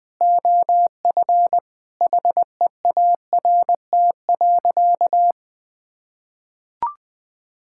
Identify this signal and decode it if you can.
Morse code, heard as a keyed tone, about 20 wpm, 700 Hz. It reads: OF HEART.
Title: Sentences with 7-Word Phrases - Spoken Beginning - 20wpm